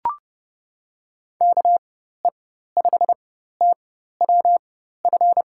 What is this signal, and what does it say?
Morse code, heard as a keyed tone, about 30 wpm, spaced as for 14 wpm, 700 Hz. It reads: KE5TWF
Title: USA Call Signs - 6 Characters - 4x Character Spacing - 30wpm